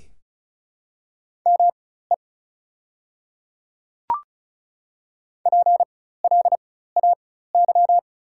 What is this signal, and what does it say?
Morse code, heard as a keyed tone, about 36 wpm, spaced as for 16 wpm, 700 Hz. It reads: ME PLAY
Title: 4x Character Spacing - Top 200 Words - 35wpm